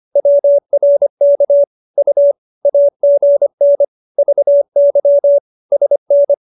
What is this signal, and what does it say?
Morse code, heard as a keyed tone, about 25 wpm, 570 Hz. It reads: WRK U AGN VY SN